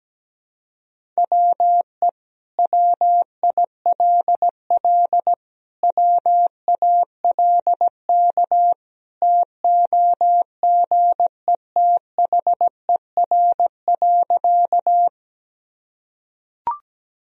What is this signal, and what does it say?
Morse code, heard as a keyed tone, about 17 wpm, 700 Hz. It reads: WE WILL WALK TOGETHER.
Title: Sentences from Top 1000 Words - Review Part 1 - 17wpm